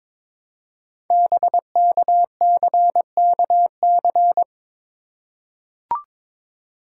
Morse code, heard as a keyed tone, about 23 wpm, 700 Hz. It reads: BKCKC
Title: Code Groups - 5 Letters - 22wpm